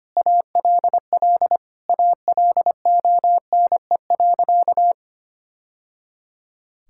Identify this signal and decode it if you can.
Morse code, heard as a keyed tone, about 25 wpm, 700 Hz. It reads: ALL ALONE.
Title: Sentences from Top 1000 Words - Review Part 5 - 25wpm